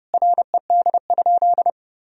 Morse code, heard as a keyed tone, about 30 wpm, 700 Hz. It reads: RED?
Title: Sentences from Top 1000 Words - Review Part 2 - 30wpm